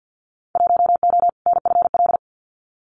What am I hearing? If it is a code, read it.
Morse code, heard as a keyed tone, about 50 wpm, 700 Hz. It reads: 1O NFL
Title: Winter Field Day Exchanges 50wpm